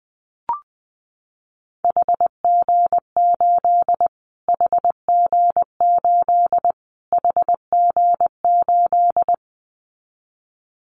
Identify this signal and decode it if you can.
Morse code, heard as a keyed tone, about 20 wpm, 700 Hz. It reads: HG8 HG8 HG8
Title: Callsign Prefixes for Top 25 DXCC - Repeated 3x - 20wpm